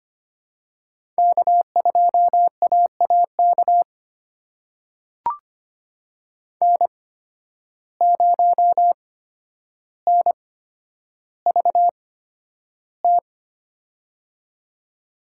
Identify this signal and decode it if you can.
Morse code, heard as a keyed tone, about 25 wpm, spaced as for 11 wpm, 700 Hz. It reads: K2AAK N0NVT
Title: USA Call Signs - 5 Characters - 8x Character Spacing - 25wpm